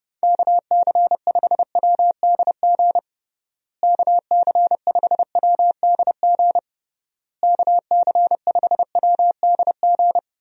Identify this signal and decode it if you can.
Morse code, heard as a keyed tone, about 30 wpm, 700 Hz. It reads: KC5WDG KC5WDG KC5WDG
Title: USA Call Signs - 6 Characters - Repeated 3x - 30wpm